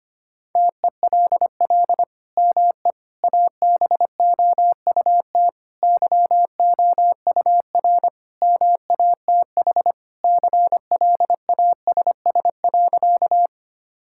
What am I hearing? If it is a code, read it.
Morse code, heard as a keyed tone, about 25 wpm, 700 Hz. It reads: TELL ME ABOUT YOUR MATH CLASS.
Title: Sentences from Top 1000 Words - Review Part 6 - 25wpm